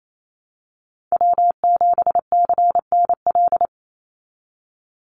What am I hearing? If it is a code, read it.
Morse code, heard as a keyed tone, about 28 wpm, 700 Hz. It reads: W7CNL
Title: NA SOTA Chaser Callsigns 28wpm